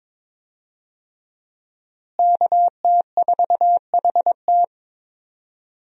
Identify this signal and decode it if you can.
Morse code, heard as a keyed tone, about 22 wpm, 700 Hz. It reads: KT4HT
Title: USA Call Signs - 5 Characters - Repeated 2x - 22wpm